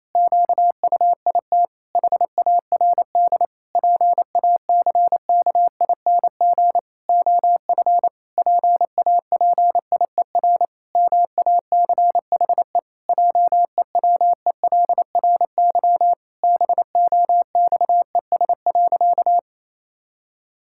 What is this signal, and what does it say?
Morse code, heard as a keyed tone, about 28 wpm, 700 Hz. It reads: QUIT HARD PACKING OF PAPIER MACHE JEWELRY BOXES.
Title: Pangram Challenge v2 - 28wpm